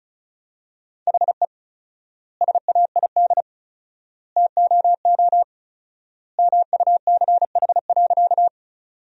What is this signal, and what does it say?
Morse code, heard as a keyed tone, about 34 wpm, 700 Hz. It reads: HE SAID TOO MUCH.
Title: 4x Word-Spacing - Sentences from Top 200 Words - 35wpm